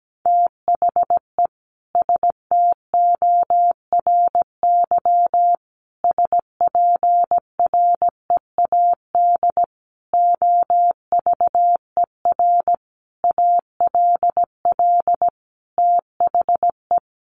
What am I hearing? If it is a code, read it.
Morse code, heard as a keyed tone, about 17 wpm, 700 Hz. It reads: THE STORY SPREAD OVER ALL THE
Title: Sentences from Top 1200 Words 17wpm